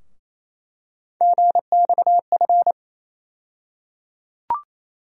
Morse code, heard as a keyed tone, about 28 wpm, 700 Hz. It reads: GXF
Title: USA Call Sign Postfixes - 3 Letters 28wpm